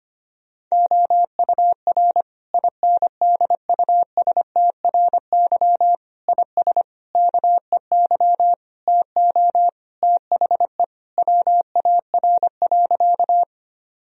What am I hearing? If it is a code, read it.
Morse code, heard as a keyed tone, about 25 wpm, 700 Hz. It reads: OUR INDUSTRY IS KEY TO THE WAR.
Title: Sentences from Top 1000 Words - Review Part 4 - 25wpm